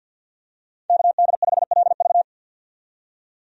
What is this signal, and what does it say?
Morse code, heard as a keyed tone, about 52 wpm, 700 Hz. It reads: KD5LV
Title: USA Call Signs - 5 Characters - 50wpm